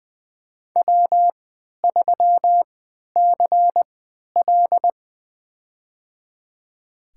Morse code, heard as a keyed tone, about 20 wpm, spaced as for 11 wpm, 700 Hz. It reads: W3CL
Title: USA Call Signs - 4 Characters - 3x Character Spacing - 20wpm